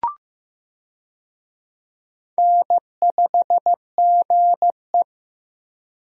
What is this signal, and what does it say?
Morse code, heard as a keyed tone, about 15 wpm, 700 Hz. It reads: N5GE
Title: USA Call Signs - 4 Characters - Repeated 2x - 15wpm